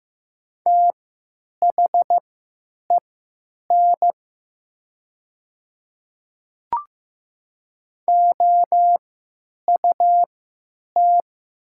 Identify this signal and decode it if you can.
Morse code, heard as a keyed tone, about 15 wpm, spaced as for 9 wpm, 700 Hz. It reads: THEN OUT
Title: 3x Character Spacing - Top 100 Words - 15wpm